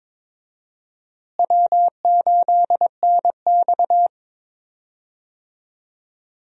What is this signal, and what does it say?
Morse code, heard as a keyed tone, about 22 wpm, 700 Hz. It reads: W8NX
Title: USA Call Signs - 4 Characters - Repeated 3x - 22wpm